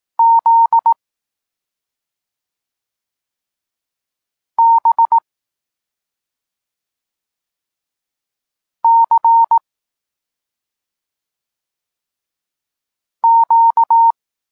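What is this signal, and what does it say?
Morse code, heard as a keyed tone, about 18 wpm, 920 Hz. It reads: Z B C Q
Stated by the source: Polish Morse lesson teaching the letter C, from SP9MOA Niepołomicki Klub Krótkofalowców.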